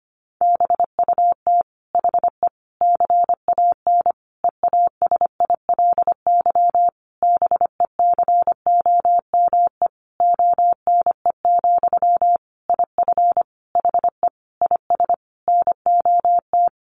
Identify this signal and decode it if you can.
Morse code, heard as a keyed tone, about 25 wpm, 700 Hz. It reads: BUT HE CAN EASILY BECOME ONE, IF HE IS NOT
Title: Sentences from Top 900 Words 25wpm